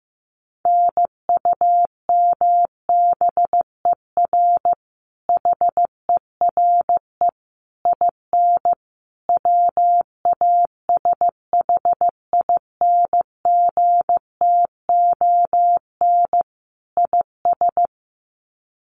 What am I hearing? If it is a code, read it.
Morse code, heard as a keyed tone, about 15 wpm, 700 Hz. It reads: NUMBER HERE IN WASHINGTON IS
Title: Sets of 5 Words Encore2 15wpm